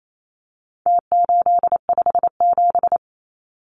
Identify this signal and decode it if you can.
Morse code, heard as a keyed tone, about 28 wpm, 700 Hz. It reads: T857
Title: Numbers - Contest Style - 4-Digit 28wpm